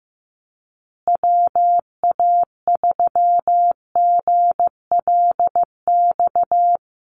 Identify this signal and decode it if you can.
Morse code, heard as a keyed tone, about 15 wpm, 700 Hz. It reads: WA3GLX
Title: USA Call Signs Repeated 2x 15wpm